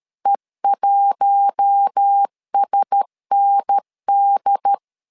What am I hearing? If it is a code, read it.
Morse code, heard as a keyed tone, about 13 wpm, 800 Hz. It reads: E1SND
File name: LEZIONI / 4 Q1S 12wpm 1802